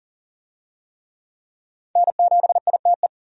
Morse code, heard as a keyed tone, about 40 wpm, 700 Hz. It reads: N7ITE
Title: USA Call Signs - 5 Characters - 40wpm